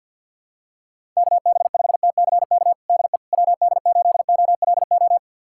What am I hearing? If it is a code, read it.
Morse code, heard as a keyed tone, about 52 wpm, 700 Hz. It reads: KB5TCK DE WD8OLO
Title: USA Callsign DE Callsign - 6 Characters - 50wpm